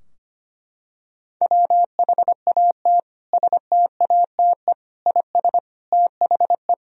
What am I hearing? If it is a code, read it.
Morse code, heard as a keyed tone, about 25 wpm, 700 Hz. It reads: WHAT STATE IS THE
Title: Sentences from Top 1000 Words - Review Part 2 - 25wpm